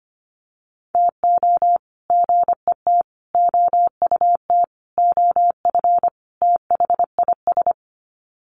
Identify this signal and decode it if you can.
Morse code, heard as a keyed tone, about 25 wpm, 700 Hz. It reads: TO GET OUT OF THIS
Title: Sets of 5 Words Encore2 25wpm